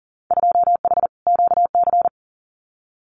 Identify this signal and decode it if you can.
Morse code, heard as a keyed tone, about 38 wpm, 700 Hz. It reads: 2H QC
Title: Winter Field Day Exchanges Encore 40wpm